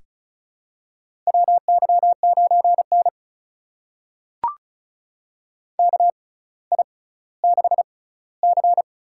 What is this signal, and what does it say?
Morse code, heard as a keyed tone, about 36 wpm, spaced as for 18 wpm, 700 Hz. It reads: WY9N KI6C
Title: USA Call Signs - 6x Character Spacing - 35wpm